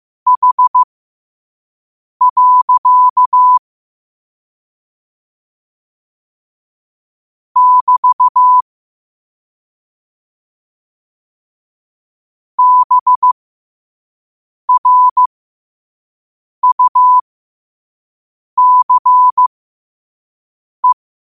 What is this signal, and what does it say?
Morse code, heard as a keyed tone, about 15 wpm, spaced as for 5 wpm, 1000 Hz. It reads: H. <BT> BRUCE
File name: 5wpm